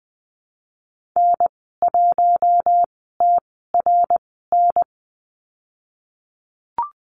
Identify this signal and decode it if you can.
Morse code, heard as a keyed tone, about 20 wpm, spaced as for 15 wpm, 700 Hz. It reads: N1TRN
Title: 055 Lesson 10g - Letter r - Call Signs 20wpm